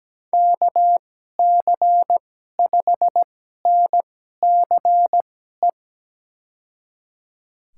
Morse code, heard as a keyed tone, about 17 wpm, spaced as for 12 wpm, 700 Hz. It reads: KC5NCE
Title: USA Call Signs - 6 Characters - 2x Character Spacing - 17wpm